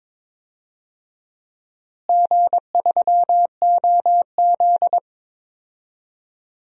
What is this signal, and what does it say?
Morse code, heard as a keyed tone, about 22 wpm, 700 Hz. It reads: G3OZ
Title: DXCC-100 Call Signs - 4 Characters - Repeated 2x - 22wpm